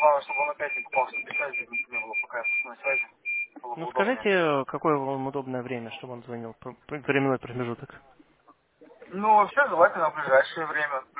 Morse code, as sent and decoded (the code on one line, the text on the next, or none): none